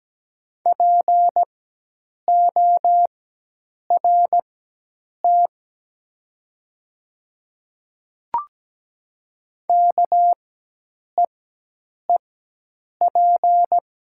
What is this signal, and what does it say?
Morse code, heard as a keyed tone, about 17 wpm, spaced as for 8 wpm, 700 Hz. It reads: PORT KEEP
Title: 4x Character Spacing - Top 500 Words - 17wpm